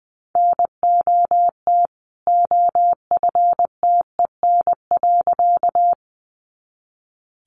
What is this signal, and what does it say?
Morse code, heard as a keyed tone, about 20 wpm, 700 Hz. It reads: NOT OFTEN.
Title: A Sound Of Thunder 20wpm